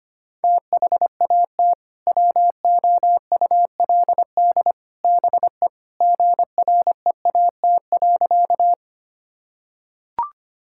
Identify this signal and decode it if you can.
Morse code, heard as a keyed tone, about 25 wpm, 700 Hz. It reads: THAT WOULD BE GREAT.
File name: practice / 25wpm / Sentences with 3-Word Phrases - 25wpm